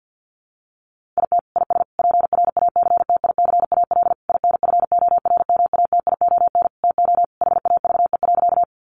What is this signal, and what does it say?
Morse code, heard as a keyed tone, about 50 wpm, 700 Hz. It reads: IT IS PRACTICAL INFORMATION TO HAVE.
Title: Sentences from Top 1400 Words 50wpm